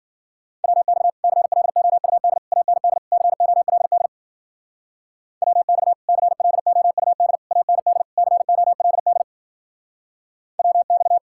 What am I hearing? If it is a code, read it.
Morse code, heard as a keyed tone, about 50 wpm, 700 Hz. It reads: WX CLOUD AND COLD WX CLOUD AND COLD WX
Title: Speed-Racing Common QSO Phrases - 40wpm